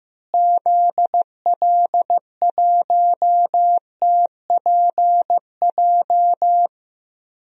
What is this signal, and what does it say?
Morse code, heard as a keyed tone, about 15 wpm, 700 Hz. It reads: ZL1TPJ
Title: DXCC-100 Call Signs - 6 Characters - Repeated 3x - 15wpm